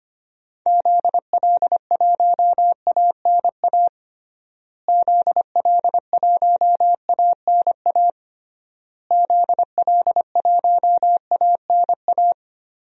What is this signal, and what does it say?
Morse code, heard as a keyed tone, about 25 wpm, 700 Hz. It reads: ZL1ANA ZL1ANA ZL1ANA
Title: DXCC-100 Call Signs - 6 Characters - Repeated 3x - 25wpm